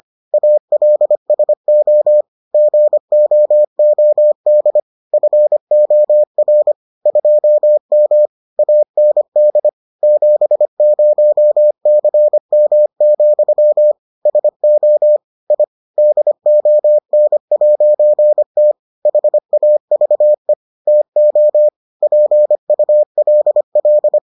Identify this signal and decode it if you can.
Morse code, heard as a keyed tone, about 25 wpm, 600 Hz. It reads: ALSO GOOD FOR 2M AND 70CM, SO I DON'T HAVE TO PULL